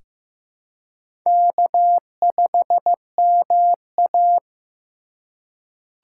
Morse code, heard as a keyed tone, about 15 wpm, 700 Hz. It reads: K5MA
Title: K1USN SST QSO Practice - 15wpm